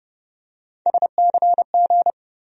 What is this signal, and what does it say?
Morse code, heard as a keyed tone, about 30 wpm, 700 Hz. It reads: SCG
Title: USA Call Sign Postfixes - 3 Letters 30wpm